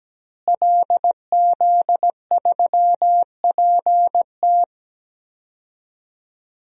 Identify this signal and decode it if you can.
Morse code, heard as a keyed tone, about 17 wpm, 700 Hz. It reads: LZ3PT
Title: DXCC-100 Call Signs - 5 Characters - Repeated 3x - 17wpm